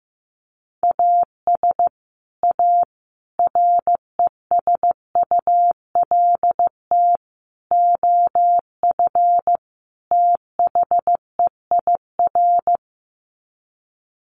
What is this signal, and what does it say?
Morse code, heard as a keyed tone, about 15 wpm, 700 Hz. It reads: AS A RESULT OF THEIR
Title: Sets of 5 Words Encore2 15wpm